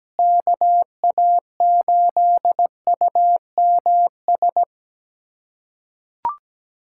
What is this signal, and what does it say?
Morse code, heard as a keyed tone, about 17 wpm, 700 Hz. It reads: KA8UMS
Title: USA Callsign DE Callsign - 6 Characters - 17wpm